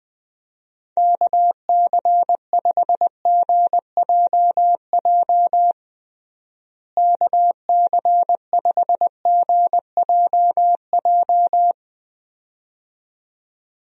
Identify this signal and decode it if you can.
Morse code, heard as a keyed tone, about 20 wpm, 700 Hz. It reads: KC5GJJ KC5GJJ
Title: USA Call Signs - 6 Characters - Repeated 2x - 20wpm